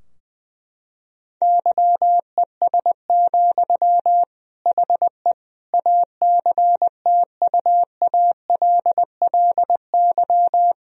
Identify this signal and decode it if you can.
Morse code, heard as a keyed tone, about 20 wpm, 700 Hz. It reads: YES, HE ACTUALLY
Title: Sentences from Top 1000 Words - Review Part 4 - 20wpm